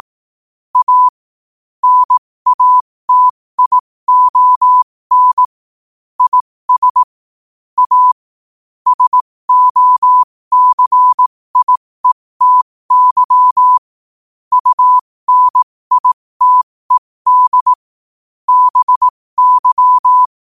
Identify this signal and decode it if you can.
Morse code, heard as a keyed tone, about 18 wpm, spaced as for 15 wpm, 1000 Hz. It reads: A NATION IS A SOCIETY UNITED BY
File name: qotdfiles / 15wpm